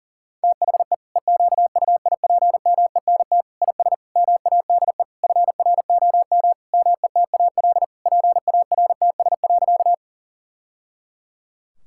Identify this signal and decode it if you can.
Morse code, heard as a keyed tone, about 40 wpm, 700 Hz. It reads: THE EQUIPMENT IS MADE FROM METAL PARTS.
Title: Sentences from Top 1000 Words - Review Part 6 - 40wpm